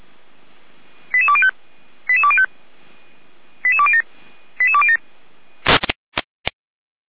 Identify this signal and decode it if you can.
Five-tone selective call (ZVEI), heard as quick runs of five tones, tones of about 70 ms. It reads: 88205 88205 88207 88207